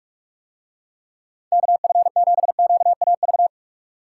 Morse code, heard as a keyed tone, about 43 wpm, 700 Hz. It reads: KF7QAV